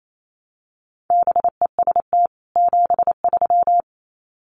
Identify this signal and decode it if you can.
Morse code, heard as a keyed tone, about 28 wpm, 700 Hz. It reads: BEST 73
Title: Common QSO Phrases 28wpm